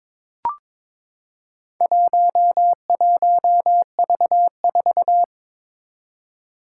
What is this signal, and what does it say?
Morse code, heard as a keyed tone, about 22 wpm, 700 Hz. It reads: 11V4
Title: Four Challenging Letter-Number Combinations 22wpm